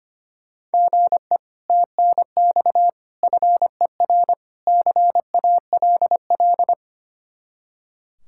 Morse code, heard as a keyed tone, about 32 wpm, 700 Hz. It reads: GE TNX FER CALL